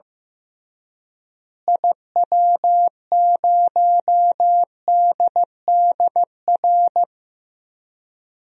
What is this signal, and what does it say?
Morse code, heard as a keyed tone, about 15 wpm, 700 Hz. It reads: IW0DDR
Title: DXCC-100 Call Signs - 6 Characters - Repeated 3x - 15wpm